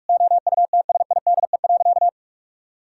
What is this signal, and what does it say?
Morse code, heard as a keyed tone, about 46 wpm, 700 Hz. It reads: OUTSIDE.
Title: Sentences from Top 1000 Words - Review Part 6 - 45wpm